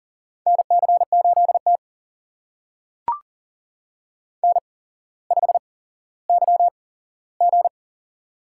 Morse code, heard as a keyed tone, about 40 wpm, spaced as for 17 wpm, 700 Hz. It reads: NC8T N5YG
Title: USA Call Signs - 4 Characters - 8x Character Spacing - 40wpm